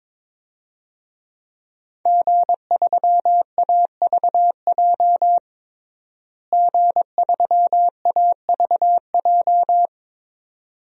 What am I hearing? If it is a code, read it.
Morse code, heard as a keyed tone, about 22 wpm, 700 Hz. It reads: G3AVJ G3AVJ
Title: DXCC-100 Call Signs - 5 Characters - Repeated 2x - 22wpm